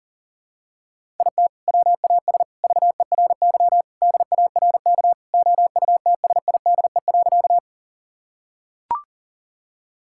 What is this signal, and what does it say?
Morse code, heard as a keyed tone, about 40 wpm, 700 Hz. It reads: IT WAS VERY DARK OUTSIDE.
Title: Sentences from Top 1000 Words - Review Part 6 - 40wpm